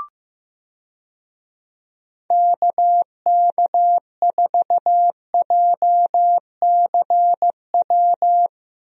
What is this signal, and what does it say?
Morse code, heard as a keyed tone, about 16 wpm, 700 Hz. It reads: KK4JCW
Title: USA Call Signs - 6 Characters - Repeated 2x - 15wpm